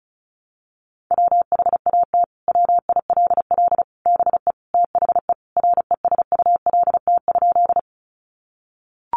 Morse code, heard as a keyed tone, about 34 wpm, 700 Hz. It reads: WHAT WILL BE THE RESULT?
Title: Sentences from Top 700 Words 35wpm